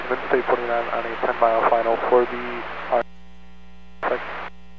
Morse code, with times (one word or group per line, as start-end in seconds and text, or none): none